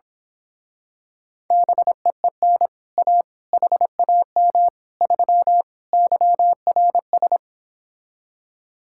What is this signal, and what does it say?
Morse code, heard as a keyed tone, about 26 wpm, 700 Hz. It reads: BEEN A HAM 3 YRS
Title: Speed-Racing Common QSO Phrases - 22wpm